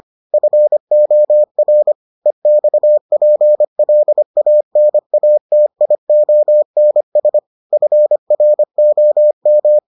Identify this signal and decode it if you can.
Morse code, heard as a keyed tone, about 25 wpm, 600 Hz. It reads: FOR EXPLANATIONS FROM